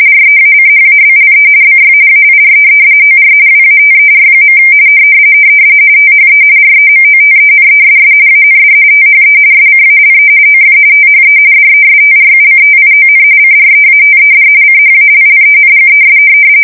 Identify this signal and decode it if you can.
RTTY, about 45.45 baud. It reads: REACHABLE BY AIR OR WATER. THE REMOTENESS AND THE SCALE OF THE DESTRUCTION CREATED CHALLENGES
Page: 170 Shift 2125/2295